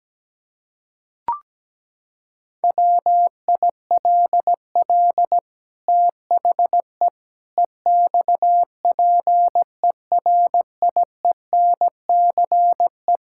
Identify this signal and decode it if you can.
Morse code, heard as a keyed tone, about 17 wpm, 700 Hz. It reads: WILL THE EXPERIENCE
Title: Sentences from Top 1000 Words - Review Part 5 - 17wpm